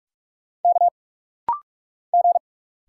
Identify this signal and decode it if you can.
Morse code, heard as a keyed tone, about 46 wpm, 700 Hz. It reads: K G
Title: Single Letter-Number Alt - Rapid-Fire 45wpm